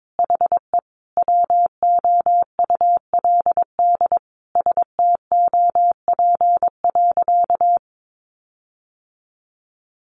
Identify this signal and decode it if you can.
Morse code, heard as a keyed tone, about 22 wpm, 700 Hz. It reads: HE WOULD STOP.
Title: Sentences from Top 1300 Words 22wpm